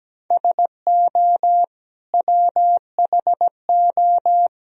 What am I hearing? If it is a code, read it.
Morse code, heard as a keyed tone, about 17 wpm, 700 Hz. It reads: SO WHO
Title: Sentences from Top 1000 Words - Review Part 3 - 17wpm